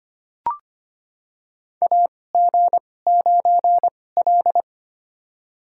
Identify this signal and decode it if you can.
Morse code, heard as a keyed tone, about 25 wpm, spaced as for 18 wpm, 700 Hz. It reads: AG9L
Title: USA Call Signs - 4 Characters - 2x Character Spacing - 25wpm